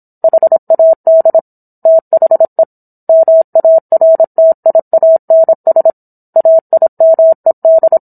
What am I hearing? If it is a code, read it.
Morse code, heard as a keyed tone, about 27 wpm, 650 Hz. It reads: HAD THE MARTIANS AIMED